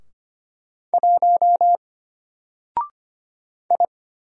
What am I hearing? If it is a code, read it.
Morse code, heard as a keyed tone, about 25 wpm, 700 Hz. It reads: 1 I
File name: Single-Letter-Number-25wpm